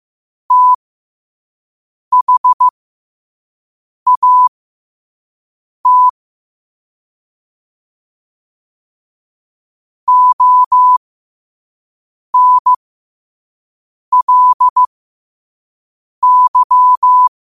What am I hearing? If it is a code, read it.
Morse code, heard as a keyed tone, about 15 wpm, spaced as for 5 wpm, 1000 Hz. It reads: THAT ONLY
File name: qotdfiles / 5wpm